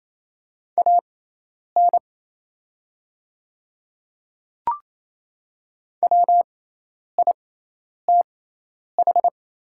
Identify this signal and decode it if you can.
Morse code, heard as a keyed tone, about 27 wpm, spaced as for 10 wpm, 700 Hz. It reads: AN WITH
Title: 6x Character Spacing - Top 100 Words - 28wpm